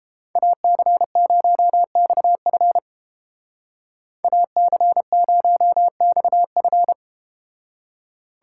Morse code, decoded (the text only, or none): AC0XF AC0XF